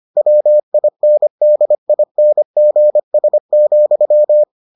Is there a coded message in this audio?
Yes: Morse code from 0.2 to 4.4 s, heard as a keyed tone, about 25 wpm, 600 Hz. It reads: WINDINGS,